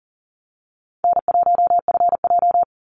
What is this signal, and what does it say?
Morse code, heard as a keyed tone, about 40 wpm, 700 Hz. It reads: N1FJ
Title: NA SOTA Chaser Callsigns 40wpm